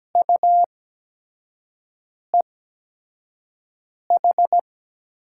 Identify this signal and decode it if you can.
Morse code, heard as a keyed tone, about 17 wpm, spaced as for 5 wpm, 700 Hz. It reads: UEH
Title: USA Call Signs - 6 Characters - 8x Character Spacing - 17wpm